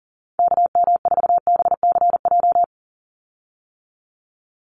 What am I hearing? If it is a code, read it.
Morse code, heard as a keyed tone, about 40 wpm, 700 Hz. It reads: KM4BCJ
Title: USA Call Signs 40wpm